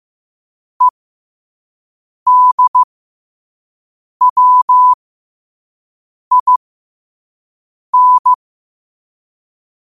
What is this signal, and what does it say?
Morse code, heard as a keyed tone, about 15 wpm, spaced as for 5 wpm, 1000 Hz. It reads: EDWIN
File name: qotdfiles / 5wpm